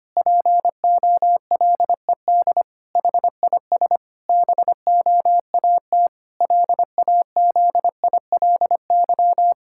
Morse code, heard as a keyed tone, about 25 wpm, 700 Hz. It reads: POLED HIS BOAT LAZILY
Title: Pangram Challenge v2 - 25wpm